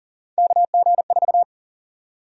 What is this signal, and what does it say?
Morse code, heard as a keyed tone, about 40 wpm, 700 Hz. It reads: KG4
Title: USA Call Sign Prefixes - Repeated 2x - 40wpm